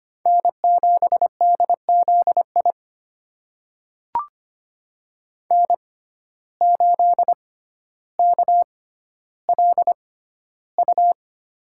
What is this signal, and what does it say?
Morse code, heard as a keyed tone, about 25 wpm, spaced as for 13 wpm, 700 Hz. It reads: N7DZI N8KLU
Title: USA Call Signs - 5 Characters - 6x Character Spacing - 25wpm